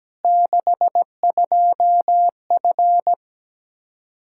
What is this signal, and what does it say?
Morse code, heard as a keyed tone, about 17 wpm, 700 Hz. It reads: 62F
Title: Practice QSO - Exchange 2 - Info - 17wpm